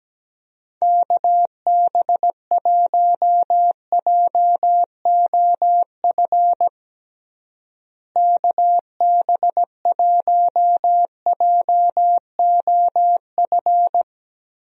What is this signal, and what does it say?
Morse code, heard as a keyed tone, about 17 wpm, 700 Hz. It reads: KB1JOF KB1JOF
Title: USA Call Signs - 6 Characters - Repeated 2x - 17wpm